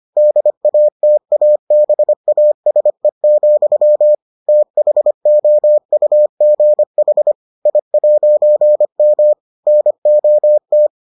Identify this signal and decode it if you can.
Morse code, heard as a keyed tone, about 25 wpm, 600 Hz. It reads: DATABASE, THOUGH I'M NOT